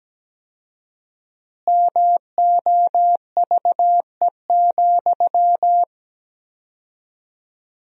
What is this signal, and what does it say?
Morse code, heard as a keyed tone, about 17 wpm, 700 Hz. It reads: MOVE,
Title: 4x Word-Spacing - Sentences from Top 300 Words - 17wpm